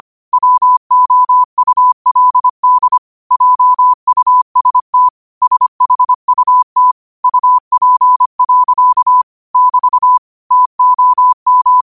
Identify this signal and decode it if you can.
Morse code, heard as a keyed tone, about 25 wpm, 1000 Hz. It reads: WOULD JUST SHUT UP. <BT> TOM
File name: qotdfiles / 25wpm